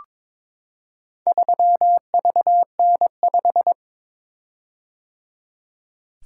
Morse code, heard as a keyed tone, about 23 wpm, 700 Hz. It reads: 3VN5
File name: practice / 22wpm / Four Letter-Number Combinations - 22wpm